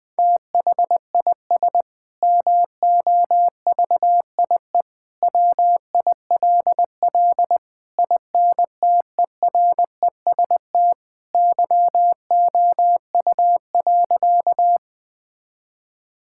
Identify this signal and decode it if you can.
Morse code, heard as a keyed tone, about 20 wpm, 700 Hz. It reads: THIS MOVIE WILL INTEREST YOU.
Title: Sentences from Top 1000 Words - Review Part 5 - 20wpm